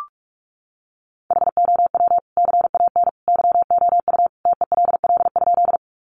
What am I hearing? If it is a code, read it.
Morse code, heard as a keyed tone, about 44 wpm, 700 Hz. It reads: HOW CAN YOU TELL?
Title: Sentences from Top 200 Words 45wpm